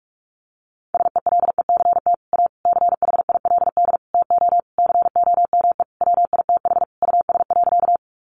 Morse code, heard as a keyed tone, about 44 wpm, 700 Hz. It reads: SELECT A CHILD TO COME WITH US.
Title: Sentences from Top 1300 Words 45wpm